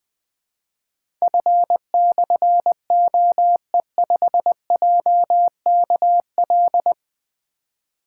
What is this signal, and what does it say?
Morse code, heard as a keyed tone, about 21 wpm, 700 Hz. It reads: F/OE5JKL
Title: Difficult Call Signs 20wpm